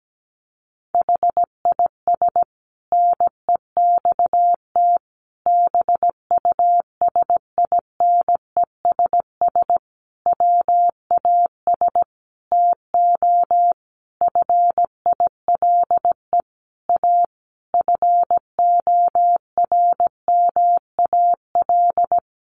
Morse code, heard as a keyed tone, about 17 wpm, 700 Hz. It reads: HIS NEXT BUSINESS WAS TO FILE A FORMAL
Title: Sentences from Top 1300 Words 17wpm